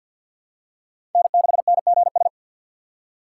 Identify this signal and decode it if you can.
Morse code, heard as a keyed tone, about 51 wpm, 690 Hz. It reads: N6NGS
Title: USA Call Signs - 5 Characters - 3x Character Spacing - 50wpm